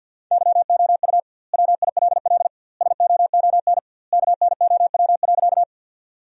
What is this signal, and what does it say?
Morse code, heard as a keyed tone, about 50 wpm, 700 Hz. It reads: YOU WILL SOON KNOW.
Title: 2x Word-Spacing - Sentences from Top 400 Words - 50wpm